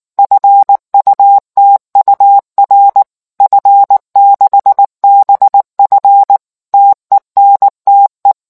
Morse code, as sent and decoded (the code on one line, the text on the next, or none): ..-. ..- - ..- .-. / ..-. -.... -... ..-. / - . -. - .
FUTUR F6BF TENTE